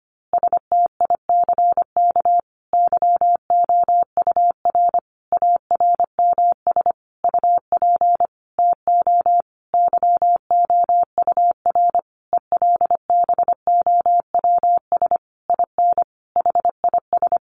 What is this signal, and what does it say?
Morse code, heard as a keyed tone, about 25 wpm, 700 Hz. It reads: STICK YOUR ARMS UP TO YOUR ELBOWS IN HIS
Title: A Sound Of Thunder 25wpm